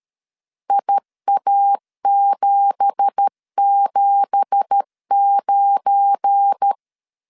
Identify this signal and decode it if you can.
Morse code, heard as a keyed tone, about 13 wpm, 800 Hz. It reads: IA779